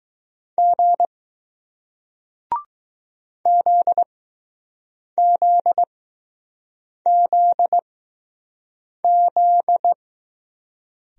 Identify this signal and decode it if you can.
Morse code, heard as a keyed tone, about 20 wpm, spaced as for 6 wpm, 700 Hz. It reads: G ZZZZ